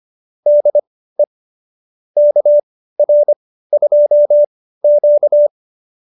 Morse code, heard as a keyed tone, about 25 wpm, spaced as for 15 wpm, 590 Hz. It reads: DE KR2Q